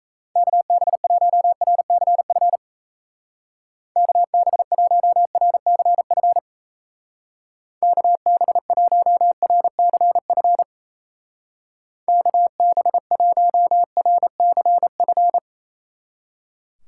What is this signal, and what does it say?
Morse code, heard as a keyed tone, about 36 wpm, 700 Hz. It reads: KB1RCF KB1RCF KB1RCF KB1RCF